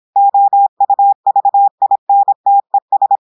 Morse code, heard as a keyed tone, about 26 wpm, 800 Hz. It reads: OUVINTES
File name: portugais texte en clair 2 vitesse 26 mots minutes